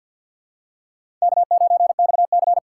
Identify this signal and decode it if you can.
Morse code, heard as a keyed tone, about 51 wpm, 700 Hz. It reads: K9XC
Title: USA Call Signs - 4x Character Spacing - 50wpm